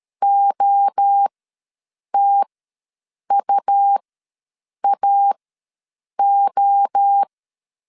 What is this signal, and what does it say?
Morse code, heard as a keyed tone, about 13 wpm, spaced as for 7 wpm, 800 Hz. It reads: OTUAO
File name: LEZIONI / 5 LBVC / 5 LBV ws 12wpm 3015